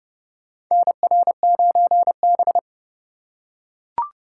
Morse code, heard as a keyed tone, about 30 wpm, 700 Hz. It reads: NR9B
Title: USA Call Signs - 3x Character Spacing - 30wpm